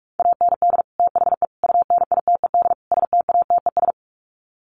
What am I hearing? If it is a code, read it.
Morse code, heard as a keyed tone, about 44 wpm, 700 Hz. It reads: AND THE UNITED STATES
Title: Sets of 4 Words 45wpm